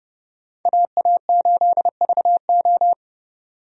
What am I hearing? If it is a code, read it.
Morse code, heard as a keyed tone, about 30 wpm, 700 Hz. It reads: AA8VO
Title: USA Call Signs - 5 Characters - 30wpm